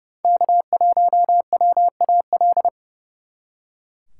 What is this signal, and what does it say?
Morse code, heard as a keyed tone, about 37 wpm, 700 Hz. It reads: K1WAL
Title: Speed Racing - USA Call Signs 30wpm